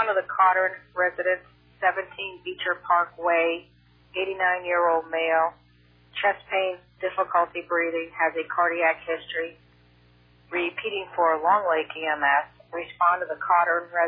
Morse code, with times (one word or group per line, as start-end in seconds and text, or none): none